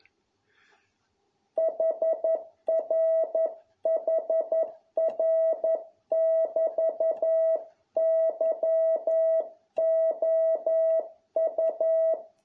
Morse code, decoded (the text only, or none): HRHR<BT>YOU